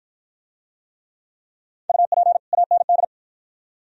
Morse code, heard as a keyed tone, about 49 wpm, 690 Hz. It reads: UP AND